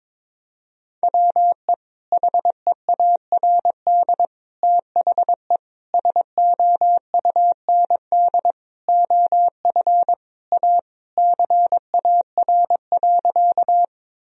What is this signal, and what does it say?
Morse code, heard as a keyed tone, about 22 wpm, 700 Hz. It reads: WE HEARD THE SOUND OF A CAR.
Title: Sentences from Top 1000 Words - Review Part 1 - 22wpm